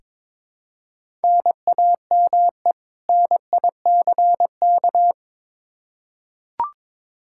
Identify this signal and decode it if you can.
Morse code, heard as a keyed tone, about 22 wpm, 700 Hz. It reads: NAME NICK
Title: Practice QSO - Exchange 1 - Beginning - 22wpm